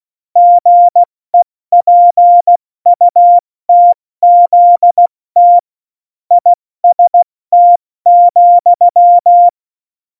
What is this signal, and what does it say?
Morse code, heard as a keyed tone, about 16 wpm, spaced as for 14 wpm, 700 Hz. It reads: GEPUTZT IST,
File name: Morsekurs / Witze / Witz 03.Dezember 2019, 70 F80